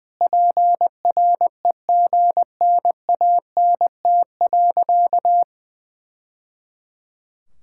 Morse code, heard as a keyed tone, about 21 wpm, 700 Hz. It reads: PREGNANT.
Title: Sentences from Top 1000 Words - Review Part 6 - 20wpm